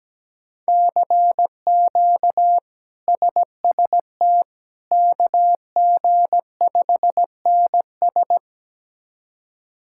Morse code, heard as a keyed tone, about 17 wpm, 700 Hz. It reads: CQ SST KG5NS
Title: K1USN SST QSO Practice - 17wpm